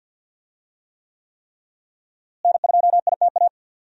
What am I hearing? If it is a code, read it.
Morse code, heard as a keyed tone, about 52 wpm, 700 Hz. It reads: N2ITA
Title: USA Call Signs - 5 Characters - Repeated 3x - 50wpm